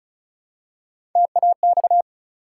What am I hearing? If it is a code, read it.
Morse code, heard as a keyed tone, about 36 wpm, 700 Hz. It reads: TAX
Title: Three Letter Words - 35wpm